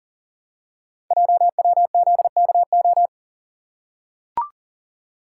Morse code, heard as a keyed tone, about 40 wpm, 690 Hz. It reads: JWZKO
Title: Code Groups - 5 Letters - 40wpm